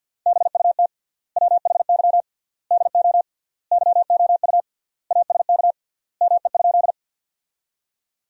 Morse code, heard as a keyed tone, about 50 wpm, 700 Hz. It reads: BUT WHY DO YOU ASK ME?
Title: 3x Word-Spacing - Sentences from Top 200 Words - 50wpm